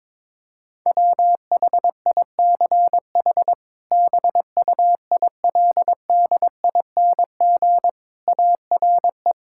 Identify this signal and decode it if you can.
Morse code, heard as a keyed tone, about 21 wpm, 700 Hz. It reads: WHICH BUILDING ARE
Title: Sentences from Top 1000 Words - Review Part 5 - 22wpm